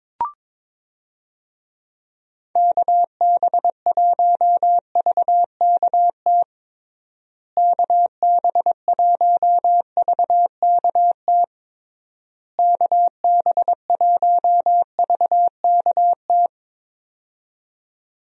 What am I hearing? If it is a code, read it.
Morse code, heard as a keyed tone, about 21 wpm, 700 Hz. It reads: KB1VKT KB1VKT KB1VKT
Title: USA Call Signs - 6 Characters - Repeated 3x - 22wpm